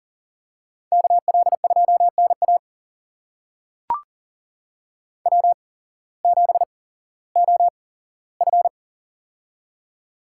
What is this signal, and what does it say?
Morse code, heard as a keyed tone, about 40 wpm, spaced as for 19 wpm, 700 Hz. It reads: KL2NA W7OF
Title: USA Call Signs - 8x Character Spacing - 40wpm